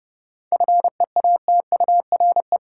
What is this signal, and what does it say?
Morse code, heard as a keyed tone, about 30 wpm, 700 Hz. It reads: FEATURE
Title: Seven Letter Words - 30wpm